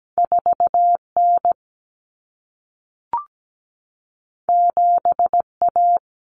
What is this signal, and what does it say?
Morse code, heard as a keyed tone, about 17 wpm, 700 Hz. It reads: 4N 7A
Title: Common Cut Numbers - 2-Digit 17wpm